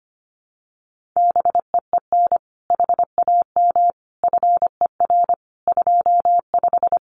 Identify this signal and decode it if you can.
Morse code, heard as a keyed tone, about 25 wpm, 700 Hz. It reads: BEEN HAM FER 25
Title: Common QSO Phrases Encore 25wpm